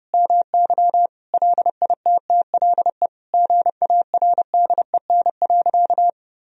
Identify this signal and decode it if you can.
Morse code, heard as a keyed tone, about 30 wpm, 700 Hz. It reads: MY LITTLE GARDEN.
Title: Sentences from Top 1000 Words - Review Part 4 - 30wpm